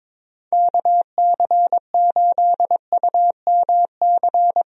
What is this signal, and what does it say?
Morse code, heard as a keyed tone, about 22 wpm, 700 Hz. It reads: KC8UMC
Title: USA Call Signs - 6 Characters - 22wpm